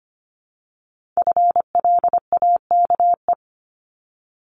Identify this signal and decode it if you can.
Morse code, heard as a keyed tone, about 25 wpm, 700 Hz. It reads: FLAKE
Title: Multi-letter Phonemes 25wpm